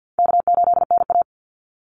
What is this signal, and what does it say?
Morse code, heard as a keyed tone, about 50 wpm, 700 Hz. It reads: K8NA
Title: Easy Call Signs Repeated 3x - 50wpm